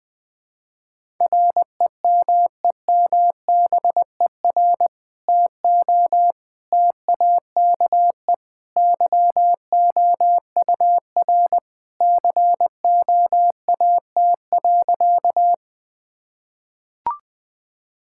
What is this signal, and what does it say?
Morse code, heard as a keyed tone, about 21 wpm, 700 Hz. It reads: REMEMBER TO TAKE YOUR COAT.
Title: Sentences with 2-Word Phrases - 20wpm